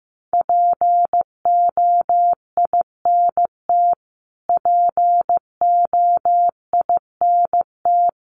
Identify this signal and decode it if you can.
Morse code, heard as a keyed tone, about 15 wpm, 700 Hz. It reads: POINT POINT
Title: Top 200 Words Repeated 15wpm